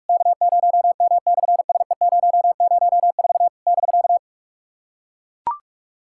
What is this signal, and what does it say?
Morse code, heard as a keyed tone, about 43 wpm, 690 Hz. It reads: K0M/SE004 <BK>
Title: SOTA QSO Practice 45wpm